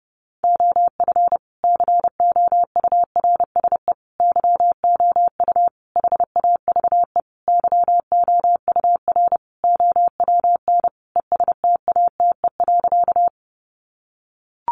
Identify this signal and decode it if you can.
Morse code, heard as a keyed tone, about 30 wpm, 700 Hz. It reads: OF COURSE YOU HAVE YOUR OWN ESTATE.
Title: Sentences from Top 1200 Words 30wpm